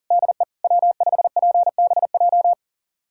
Morse code, heard as a keyed tone, about 40 wpm, 700 Hz. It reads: DE W5PBJ
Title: USA Callsign DE Callsign - 5 Characters - 40wpm